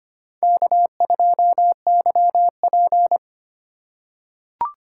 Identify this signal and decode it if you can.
Morse code, heard as a keyed tone, about 25 wpm, 700 Hz. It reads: K2YP
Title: USA Call Signs - 4x Character Spacing - 25wpm